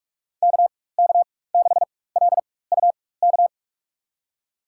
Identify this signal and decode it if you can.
Morse code, heard as a keyed tone, about 46 wpm, spaced as for 21 wpm, 700 Hz. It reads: KK6LUK
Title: USA Call Signs - 6 Characters - 4x Character Spacing - 45wpm